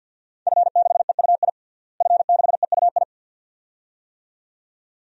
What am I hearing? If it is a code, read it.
Morse code, heard as a keyed tone, about 48 wpm, 700 Hz. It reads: F6EUI F6EUI
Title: DXCC-100 Call Signs - 5 Characters - Repeated 2x - 50wpm